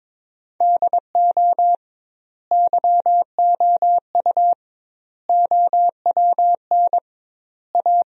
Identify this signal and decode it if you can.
Morse code, heard as a keyed tone, about 22 wpm, 700 Hz. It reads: DO YOU OWN A
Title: 2x Word-Spacing - Sentences from Top 400 Words - 22wpm